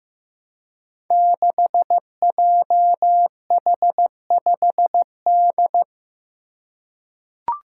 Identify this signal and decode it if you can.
Morse code, heard as a keyed tone, about 15 wpm, 700 Hz. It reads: 6JH5D
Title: Five Challenging Letter-Number Combinations 15wpm